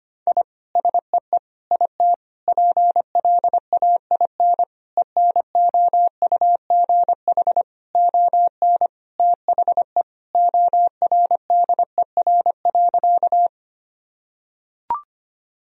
Morse code, heard as a keyed tone, about 25 wpm, 700 Hz. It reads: I SEE IT PLAIN ENOUGH ON THE ORDER.
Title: Sentences from Top 1000 Words - Review Part 2 - 25wpm